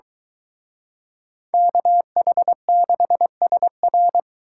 Morse code, heard as a keyed tone, about 23 wpm, 700 Hz. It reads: KH6SR